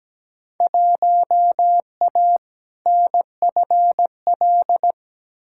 Morse code, heard as a keyed tone, about 17 wpm, 700 Hz. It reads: 1A NFL
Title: Field Day Exchanges - 17wpm